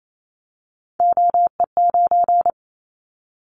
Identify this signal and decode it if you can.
Morse code, heard as a keyed tone, about 28 wpm, 700 Hz. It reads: OE9
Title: Callsign Prefixes for Top 50 DXCC - 28wpm